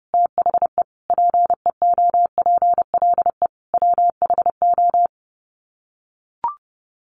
Morse code, heard as a keyed tone, about 28 wpm, 700 Hz. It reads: THE PEOPLE WHO
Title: Sets of 4 Words 30wpm